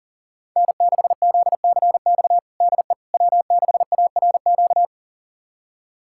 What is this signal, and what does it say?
Morse code, heard as a keyed tone, about 40 wpm, 700 Hz. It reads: N6ZCX DE W6ARQ
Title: USA Callsign DE Callsign - 5 Characters - 40wpm